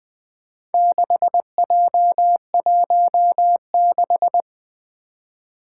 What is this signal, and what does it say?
Morse code, heard as a keyed tone, about 20 wpm, 700 Hz. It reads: 6J16
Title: Four Challenging Letter-Number Combinations 20wpm